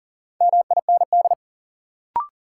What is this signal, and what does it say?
Morse code, heard as a keyed tone, about 38 wpm, 700 Hz. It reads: MIND
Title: Sets of 5 Words - New Word Drill - 40wpm